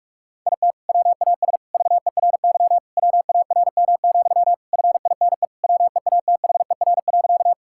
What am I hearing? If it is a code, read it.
Morse code, heard as a keyed tone, about 46 wpm, 700 Hz. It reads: IT WAS VERY WARM, FINE WEATHER.
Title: Sentences from Top 1000 Words - Review Part 5 - 45wpm